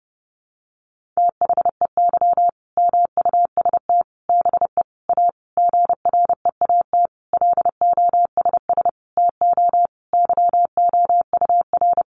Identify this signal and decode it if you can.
Morse code, heard as a keyed tone, about 30 wpm, 700 Hz. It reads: THEY MUST BE A GREAT LOSS TO YOUR
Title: Sentences from Top 1400 Words 30wpm